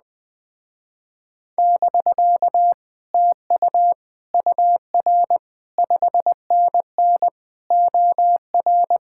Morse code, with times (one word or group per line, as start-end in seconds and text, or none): 1.6-2.7 <BK>
3.1-3.9 TU
4.3-5.4 UR
5.8-7.3 5NN
7.7-9.0 OR